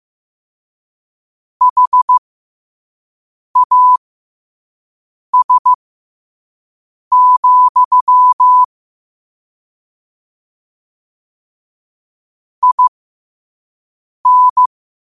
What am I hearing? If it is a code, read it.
Morse code, heard as a keyed tone, about 15 wpm, spaced as for 5 wpm, 1000 Hz. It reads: HAS, IN
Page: Quotes for Wed, 13 Aug 2025 in Morse Code at 5 words per minute.